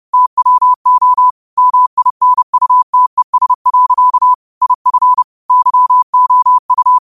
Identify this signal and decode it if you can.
Morse code, heard as a keyed tone, about 30 wpm, 1000 Hz. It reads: TWO MINUTES. IF YOU
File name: qotdfiles / 30wpm